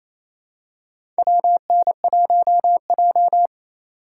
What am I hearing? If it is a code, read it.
Morse code, heard as a keyed tone, about 28 wpm, 700 Hz. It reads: WN1J
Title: USA Call Signs - 4 Characters - 4x Character Spacing - 28wpm